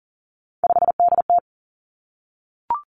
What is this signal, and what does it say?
Morse code, heard as a keyed tone, about 42 wpm, 700 Hz. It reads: 5DT
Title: Three Challenging Letter-Number Combinations 40wpm